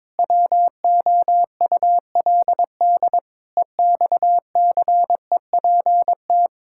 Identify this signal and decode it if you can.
Morse code, heard as a keyed tone, about 22 wpm, 700 Hz. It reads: WOULD EXCEPT
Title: Sentences from Top 1600 Words 22wpm